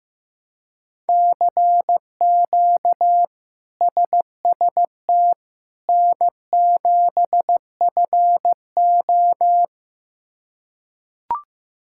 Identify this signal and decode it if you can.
Morse code, heard as a keyed tone, about 15 wpm, 700 Hz. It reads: CQ SST N7FO
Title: K1USN SST QSO Practice - 15wpm